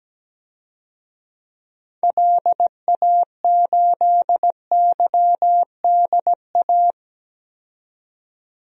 Morse code, heard as a keyed tone, about 17 wpm, 700 Hz. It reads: LA8YDA
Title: DXCC-100 Call Signs - 6 Characters - 17wpm